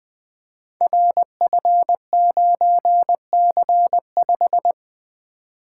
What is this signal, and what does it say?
Morse code, heard as a keyed tone, about 20 wpm, 700 Hz. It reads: RF9C5
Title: Code Groups - 5 Letter:Numbers - 20wpm